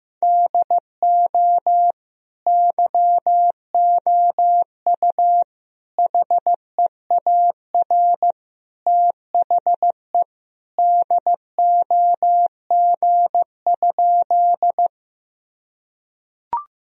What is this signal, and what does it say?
Morse code, heard as a keyed tone, about 15 wpm, 700 Hz. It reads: DO YOU HEAR THE DOG?
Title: Sentences from Top 1000 Words - Review Part 2 - 15wpm